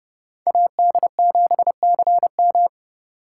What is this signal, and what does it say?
Morse code, heard as a keyed tone, about 29 wpm, 700 Hz. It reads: AD7CM